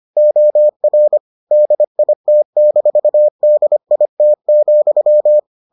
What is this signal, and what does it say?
Morse code, heard as a keyed tone, about 25 wpm, 600 Hz. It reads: OR DIT-DIT,